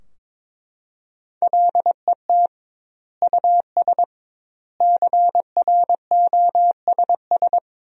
Morse code, heard as a keyed tone, about 22 wpm, 700 Hz. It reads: LET US CROSS
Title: 2x Word-Spacing - Sentences from Top 300 Words - 22wpm